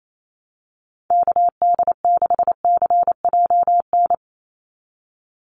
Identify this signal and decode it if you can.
Morse code, heard as a keyed tone, about 28 wpm, 700 Hz. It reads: KD6CJN
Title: USA Call Signs 28wpm